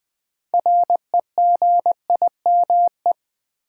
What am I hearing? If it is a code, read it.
Morse code, heard as a keyed tone, about 20 wpm, 700 Hz. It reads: REGIME
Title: 6 Letter Words - First 3 Letters Given - 20wpm